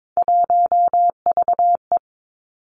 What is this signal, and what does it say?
Morse code, heard as a keyed tone, about 22 wpm, 700 Hz. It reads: 1VE
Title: Three Challenging Letter-Number Combinations 22wpm